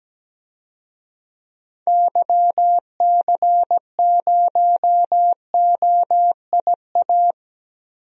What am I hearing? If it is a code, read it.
Morse code, heard as a keyed tone, about 17 wpm, 700 Hz. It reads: YC0OIA
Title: DXCC-100 Call Signs - 6 Characters - Repeated 2x - 17wpm